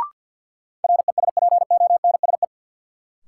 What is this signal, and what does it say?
Morse code, heard as a keyed tone, about 53 wpm, 700 Hz. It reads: RESPONSE
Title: Sentences from Top 1400 Words - New Word Drill Alt - 50wpm